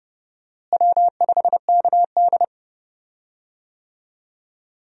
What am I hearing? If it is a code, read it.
Morse code, heard as a keyed tone, about 30 wpm, 700 Hz. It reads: W5KD
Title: USA Call Signs - 4 Characters - Repeated 3x - 30wpm